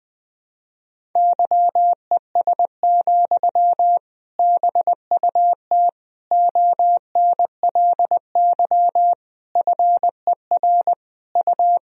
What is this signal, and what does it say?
Morse code, heard as a keyed tone, about 20 wpm, 700 Hz. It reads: YES, BUT ONLY FER U
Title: Sentences from Top 500 Words and Abbreviations 20wpm